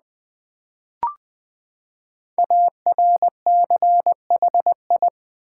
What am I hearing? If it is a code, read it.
Morse code, heard as a keyed tone, about 20 wpm, 700 Hz. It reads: ARCHI
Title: Build A Word - 5 Letters At A Time - 12-Letter Words - 20wpm